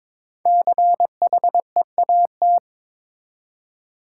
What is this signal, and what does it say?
Morse code, heard as a keyed tone, about 22 wpm, 700 Hz. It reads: CHEAT
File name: 5 Letter Words - First 3 Letters Given - 22wpm